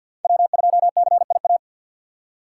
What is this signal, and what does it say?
Morse code, heard as a keyed tone, about 62 wpm, 700 Hz. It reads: W2CIA
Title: Speed Racing - USA 1x3 Call Signs 50wpm